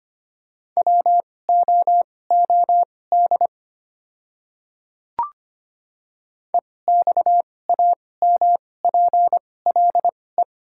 Morse code, heard as a keyed tone, about 25 wpm, spaced as for 18 wpm, 700 Hz. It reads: WOOD EXAMPLE
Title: 2x Character Spacing - Top 300 Words - 25wpm